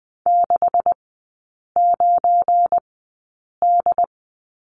Morse code, heard as a keyed tone, about 20 wpm, spaced as for 8 wpm, 700 Hz. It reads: 69D
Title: 212 Lesson 34e - Number 8 - Three Characters 20wpm